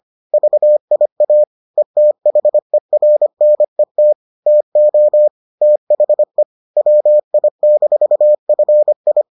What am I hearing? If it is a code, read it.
Morse code, heard as a keyed tone, about 25 wpm, 600 Hz. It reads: VIA ETHERNET TO THE WI-FI